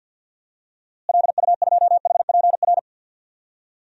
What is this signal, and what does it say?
Morse code, heard as a keyed tone, about 52 wpm, 700 Hz. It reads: LU2HPR
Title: DXCC-100 Call Signs - 6 Characters - Repeated 2x - 50wpm